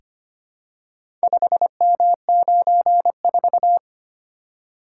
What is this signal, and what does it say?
Morse code, heard as a keyed tone, about 25 wpm, 700 Hz. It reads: 5M94
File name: practice / 25wpm / Four Letter-Number Combinations - 25wpm